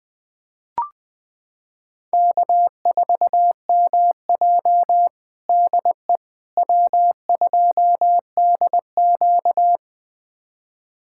Morse code, heard as a keyed tone, about 20 wpm, 700 Hz. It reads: K4MJ DE W2DQ
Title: USA Callsign DE Callsign - 4 Characters - 20wpm